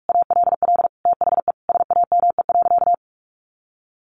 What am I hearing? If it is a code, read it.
Morse code, heard as a keyed tone, about 46 wpm, 700 Hz. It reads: ALL THE SAME.
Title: Sentences from Top 1000 Words 45wpm